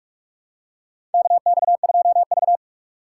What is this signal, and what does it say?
Morse code, heard as a keyed tone, about 46 wpm, 700 Hz. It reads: KX2V
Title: USA Call Signs - 3x Character Spacing - 45wpm